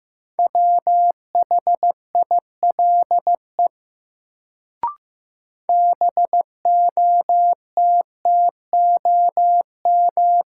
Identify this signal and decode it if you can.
Morse code, heard as a keyed tone, about 15 wpm, 700 Hz. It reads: WHILE BOTTOM
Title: Sets of 3 Words - New Word Drill - 15wpm